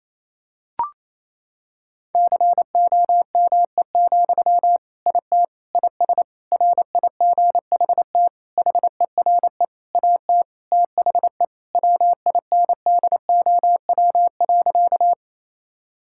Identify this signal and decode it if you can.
Morse code, heard as a keyed tone, about 28 wpm, 700 Hz. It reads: COME, IT IS RIGHT HERE AT THE WINDOW.
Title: Sentences from Top 1600 Words 28wpm